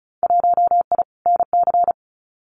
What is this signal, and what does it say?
Morse code, heard as a keyed tone, about 36 wpm, 700 Hz. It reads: 1I NC
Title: Winter Field Day Exchanges Encore 35wpm